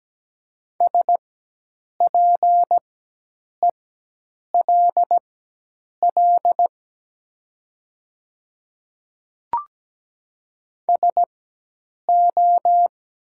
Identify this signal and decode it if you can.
Morse code, heard as a keyed tone, about 17 wpm, spaced as for 8 wpm, 700 Hz. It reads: SPELL SO
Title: 4x Character Spacing - Top 200 Words - 17wpm